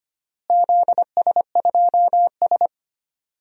Morse code, heard as a keyed tone, about 25 wpm, 700 Hz. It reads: ZS2S
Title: DXCC-100 Call Signs - 4 Characters - Repeated 2x - 25wpm